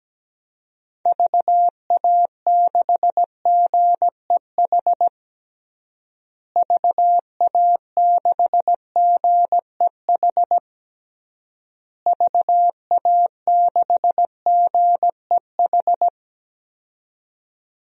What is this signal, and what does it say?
Morse code, heard as a keyed tone, about 17 wpm, 700 Hz. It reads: VA6GEH VA6GEH VA6GEH
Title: DXCC-100 Call Signs - 6 Characters - Repeated 3x - 17wpm